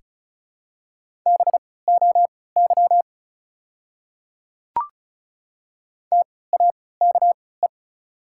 Morse code, heard as a keyed tone, about 36 wpm, spaced as for 20 wpm, 700 Hz. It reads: BOY TAKE
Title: 3x Character Spacing - Top 200 Words - 35wpm